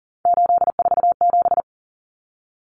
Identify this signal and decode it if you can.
Morse code, heard as a keyed tone, about 40 wpm, 700 Hz. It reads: 847
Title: Numbers - 3-Digit 40wpm